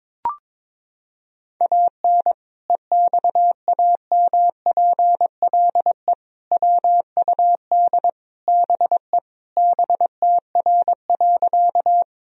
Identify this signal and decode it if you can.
Morse code, heard as a keyed tone, about 22 wpm, 700 Hz. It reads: AN EXAMPLE WUD BE BTR.
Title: Sentences from Top 400 Words and Abbreviations 22wpm